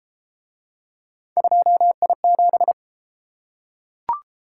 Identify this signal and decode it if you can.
Morse code, heard as a keyed tone, about 28 wpm, 700 Hz. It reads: 2I7